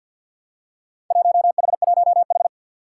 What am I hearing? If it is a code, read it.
Morse code, heard as a keyed tone, about 47 wpm, 700 Hz. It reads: 1H1H
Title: Four Challenging Letter-Number Combinations 50wpm